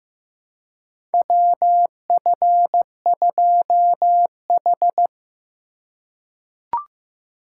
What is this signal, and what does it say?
Morse code, heard as a keyed tone, about 15 wpm, 700 Hz. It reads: WF2H
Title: USA Call Signs - 4 Characters - 4x Character Spacing - 15wpm